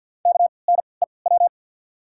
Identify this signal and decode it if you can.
Morse code, heard as a keyed tone, about 53 wpm, spaced as for 29 wpm, 690 Hz. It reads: KNEW
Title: 3x Character Spacing - Top 500 Words - 50wpm